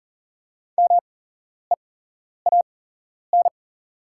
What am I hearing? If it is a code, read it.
Morse code, heard as a keyed tone, about 40 wpm, spaced as for 11 wpm, 690 Hz. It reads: MEAN
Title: 8x Character Spacing - Top 300 Words - 40wpm